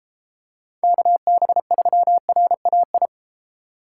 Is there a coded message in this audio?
Yes: Morse code from 0.8 to 3.1 s, heard as a keyed tone, about 33 wpm, 700 Hz. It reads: KB3RAI